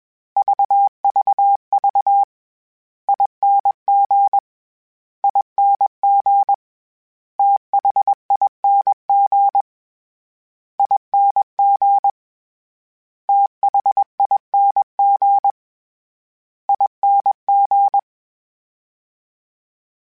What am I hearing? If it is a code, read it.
Morse code, heard as a keyed tone, about 21 wpm, 800 Hz. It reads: VVV ING ING THING ING THING ING
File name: Special INGs-mp3s / Special INGs-mp3s / SlowToFaster ing @20